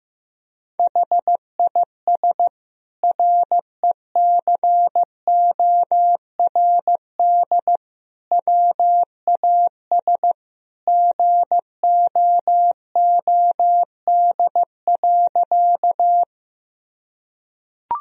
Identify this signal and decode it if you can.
Morse code, heard as a keyed tone, about 16 wpm, 700 Hz. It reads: HIS RECORD WAS GOOD.
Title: Sentences from Top 1000 Words - Review Part 3 - 15wpm